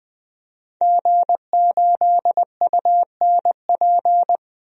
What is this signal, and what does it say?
Morse code, heard as a keyed tone, about 21 wpm, 700 Hz. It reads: G8UNP
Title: DXCC-100 Call Signs - 5 Characters - 20wpm